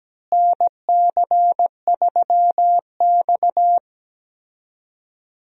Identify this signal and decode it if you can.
Morse code, heard as a keyed tone, about 17 wpm, 700 Hz. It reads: NC3X
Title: USA Call Signs - 4 Characters - Repeated 2x - 17wpm